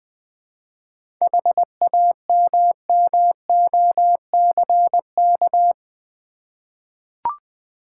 Morse code, heard as a keyed tone, about 20 wpm, 700 Hz. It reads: HAMMOCK
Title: Easy Words with Hints - 20wpm